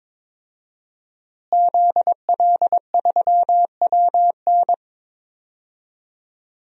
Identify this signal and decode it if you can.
Morse code, heard as a keyed tone, about 22 wpm, 700 Hz. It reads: ZL3WN
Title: DXCC-100 Call Signs - 5 Characters - Repeated 3x - 22wpm